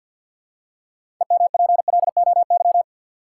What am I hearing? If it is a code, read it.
Morse code, heard as a keyed tone, about 50 wpm, 700 Hz. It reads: EMPLOY
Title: Six Letter Words - 50wpm